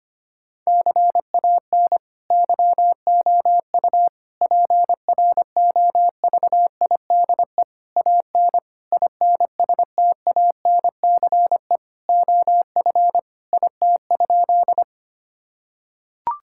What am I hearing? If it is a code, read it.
Morse code, heard as a keyed tone, about 25 wpm, 700 Hz. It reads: CAN YOU PROVIDE AN INSTANCE OF IT?
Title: Sentences from Top 1000 Words - Review Part 5 - 25wpm